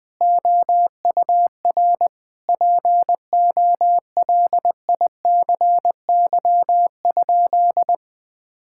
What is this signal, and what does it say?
Morse code, heard as a keyed tone, about 20 wpm, 700 Hz. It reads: OUR POLICY?
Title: Sentences from Top 1000 Words - Review Part 5 - 20wpm